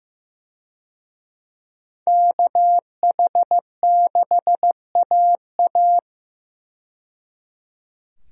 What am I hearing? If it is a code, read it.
Morse code, heard as a keyed tone, about 15 wpm, 700 Hz. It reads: KH6AA
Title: DXCC-100 Call Signs - 5 Characters - 15wpm